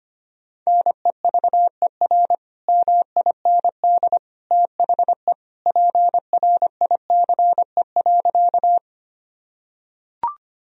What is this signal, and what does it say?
Morse code, heard as a keyed tone, about 25 wpm, 700 Hz. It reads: NEVER MIND THE PRICE.
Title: Sentences from Top 1000 Words - Review Part 4 - 25wpm